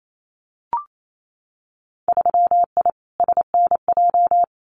Morse code, heard as a keyed tone, about 28 wpm, 700 Hz. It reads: 3I SNJ
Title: Winter Field Day Exchanges Encore 28wpm